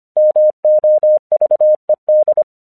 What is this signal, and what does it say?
Morse code, heard as a keyed tone, about 26 wpm, 610 Hz. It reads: MOVED
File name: 1 English 25wpm / AQC28